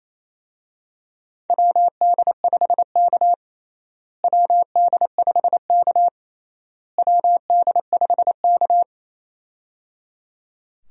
Morse code, heard as a keyed tone, about 27 wpm, 700 Hz. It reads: WD5K WD5K WD5K
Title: USA Call Signs - 4 Characters - Repeated 3x - 28wpm